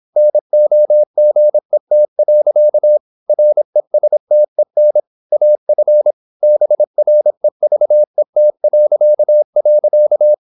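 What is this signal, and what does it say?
Morse code, heard as a keyed tone, about 26 wpm, 600 Hz. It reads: NOGET. RESTEN AF BREVET..
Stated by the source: High-speed (QRQ) Morse at 26 wpm.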